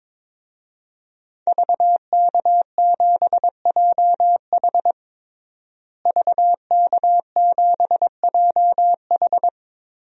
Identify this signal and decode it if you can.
Morse code, heard as a keyed tone, about 21 wpm, 700 Hz. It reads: VK7JH VK7JH
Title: DXCC-100 Call Signs - 5 Characters - Repeated 2x - 22wpm